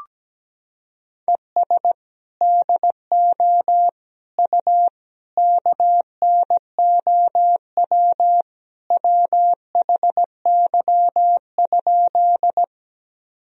Morse code, heard as a keyed tone, about 17 wpm, 700 Hz. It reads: ES DO U KNOW WHY?
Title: Sentences from Top 200 Words and Abbreviations 17wpm